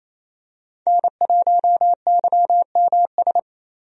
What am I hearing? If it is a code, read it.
Morse code, heard as a keyed tone, about 28 wpm, 700 Hz. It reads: N1YMS
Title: USA Call Signs - 5 Characters - 28wpm